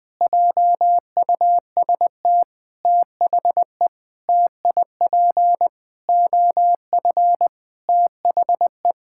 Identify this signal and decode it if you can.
Morse code, heard as a keyed tone, about 20 wpm, 700 Hz. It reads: JUST THE TIP OF THE
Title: Sentences with 7-Word Phrases - 20wpm